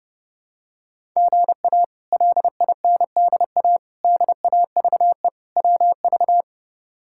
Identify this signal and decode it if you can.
Morse code, heard as a keyed tone, about 32 wpm, 700 Hz. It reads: GA LINDA DAVE WV
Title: K1USN SST QSO Practice - 30wpm